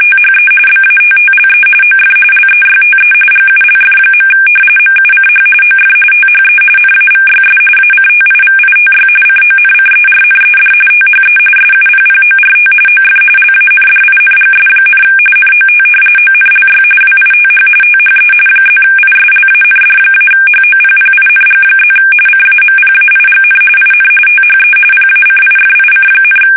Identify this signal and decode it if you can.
RTTY, about 45.45 baud. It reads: THE BATTLE OF LEYTE GULF. MUCH OF THE NAVYS ACTIVITY CONCERNED THE SUPPORT OF LANDINGS, NOT ONLY WITH THE +ISLAND-HOPPING+ CAMPAIGN IN THE PACIFIC,
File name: fsk850 / usnavy1